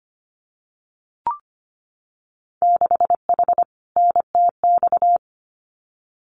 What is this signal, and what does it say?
Morse code, heard as a keyed tone, about 25 wpm, 700 Hz. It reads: 6H NTX
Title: Winter Field Day Exchanges Encore 25wpm